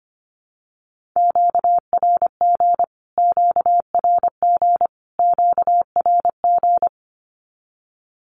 Morse code, heard as a keyed tone, about 25 wpm, 700 Hz. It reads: QRG QRG QRG
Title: Q-Codes Repeated 3x 25wpm